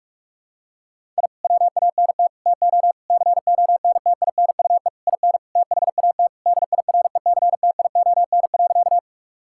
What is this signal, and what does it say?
Morse code, heard as a keyed tone, about 46 wpm, 700 Hz. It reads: I WANT TO CONTINUE IN THAT DIRECTION.
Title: Sentences from Top 1000 Words - Review Part 6 - 45wpm